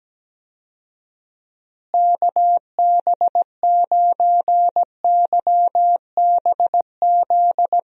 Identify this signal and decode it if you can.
Morse code, heard as a keyed tone, about 17 wpm, 700 Hz. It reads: KB9YBZ
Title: USA Call Signs - 6 Characters - Repeated 3x - 17wpm